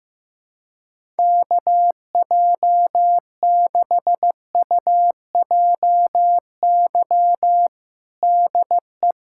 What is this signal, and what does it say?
Morse code, heard as a keyed tone, about 16 wpm, 700 Hz. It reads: KJ6UJY DE
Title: USA Callsign DE Callsign - 6 Characters - 15wpm